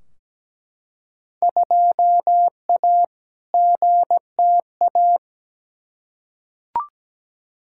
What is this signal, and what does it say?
Morse code, heard as a keyed tone, about 17 wpm, 700 Hz. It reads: 2A GTA
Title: Field Day Exchanges Encore - 17wpm